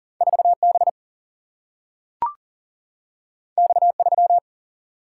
Morse code, heard as a keyed tone, about 40 wpm, 700 Hz. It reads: 4 B X 3
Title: Single-Letter Single-Number 40wpm